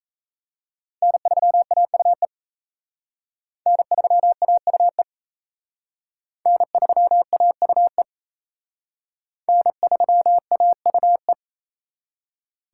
Morse code, heard as a keyed tone, about 34 wpm, 700 Hz. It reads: N3AUE N3AUE N3AUE N3AUE
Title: Speed Racing - USA Call Signs 28wpm